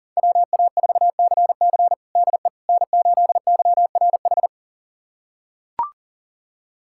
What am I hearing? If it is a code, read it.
Morse code, heard as a keyed tone, about 37 wpm, 700 Hz. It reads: WA4CC DE N8YRH
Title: USA Callsign DE Callsign - 5 Characters - 40wpm